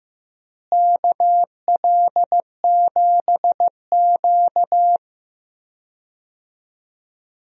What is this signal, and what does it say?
Morse code, heard as a keyed tone, about 15 wpm, 700 Hz. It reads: KL7Q
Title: DXCC-100 Call Signs - 4 Characters - Repeated 3x - 15wpm